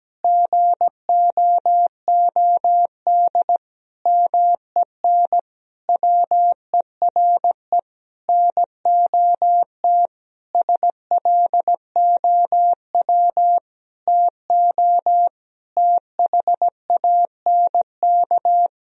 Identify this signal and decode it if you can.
Morse code, heard as a keyed tone, about 17 wpm, 700 Hz. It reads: GOOD MEN WERE NOT SLOW TO THANK
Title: Sentences from Top 1000 Words - Review Part 2 - 17wpm